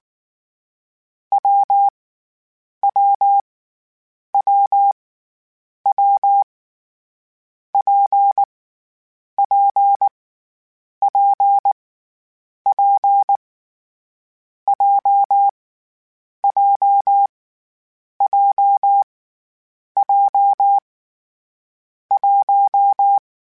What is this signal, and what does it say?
Morse code, heard as a keyed tone, about 19 wpm, 800 Hz. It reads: W W W W P P P P J J J J 1
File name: Variable Speed / Variable Speed-mp3s / WPJ WPJ1 sp @18